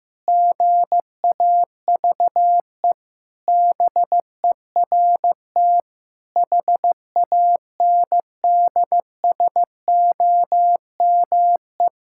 Morse code, heard as a keyed tone, about 15 wpm, 700 Hz. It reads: GAVE BERT HANDSOME
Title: Pangram Challenge v2 - 15wpm